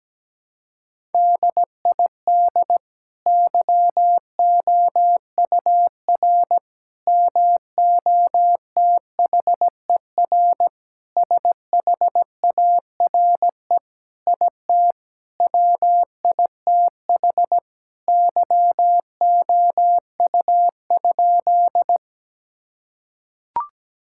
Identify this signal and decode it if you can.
Morse code, heard as a keyed tone, about 17 wpm, 700 Hz. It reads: DID YOUR MOTHER SHARE IT WITH YOU?
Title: Sentences from Top 1000 Words - Review Part 6 - 17wpm